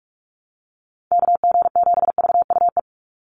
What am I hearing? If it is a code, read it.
Morse code, heard as a keyed tone, about 44 wpm, 700 Hz. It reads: KG7VUE
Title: USA Call Signs 45wpm